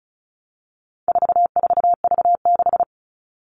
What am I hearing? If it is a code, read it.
Morse code, heard as a keyed tone, about 34 wpm, 700 Hz. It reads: 44V6
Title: All Cut Numbers - 4-Digit 35wpm